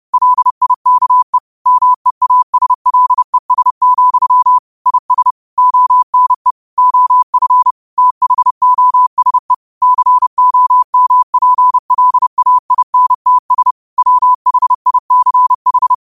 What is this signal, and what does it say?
Morse code, heard as a keyed tone, about 30 wpm, 1000 Hz. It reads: LIKE MEASLES, IS ONE OF THOSE COMPLAINTS WHICH